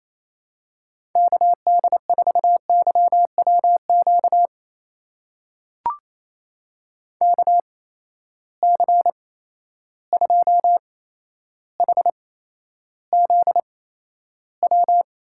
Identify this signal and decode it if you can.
Morse code, heard as a keyed tone, about 28 wpm, spaced as for 12 wpm, 700 Hz. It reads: KD4YWQ KC2HZW
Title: USA Call Signs - 6 Characters - 8x Character Spacing - 28wpm